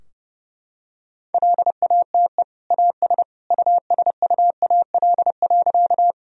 Morse code, heard as a keyed tone, about 30 wpm, 700 Hz. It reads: LATE AS USUAL.
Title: Sentences from Top 1000 Words - Review Part 2 - 30wpm